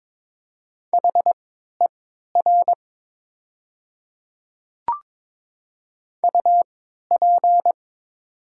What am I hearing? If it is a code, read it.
Morse code, heard as a keyed tone, about 22 wpm, spaced as for 12 wpm, 700 Hz. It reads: HER UP
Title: 3x Character Spacing - Top 300 Words - 22wpm